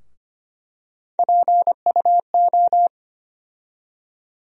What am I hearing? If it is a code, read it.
Morse code, heard as a keyed tone, about 25 wpm, 700 Hz. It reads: PUO
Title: USA Call Sign Postfixes - 3 Letters 25wpm